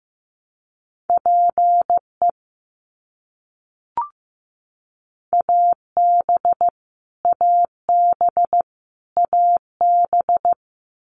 Morse code, heard as a keyed tone, about 16 wpm, 700 Hz. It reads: PE AB AB AB
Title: Canadian Provinces and Territories - Repeated 3x - 15wpm